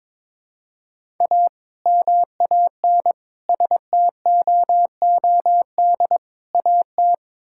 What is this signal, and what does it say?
Morse code, heard as a keyed tone, about 22 wpm, 700 Hz. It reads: A MAN STOOD AT
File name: practice / 22wpm / Sentences with 2-Word Phrases - 22wpm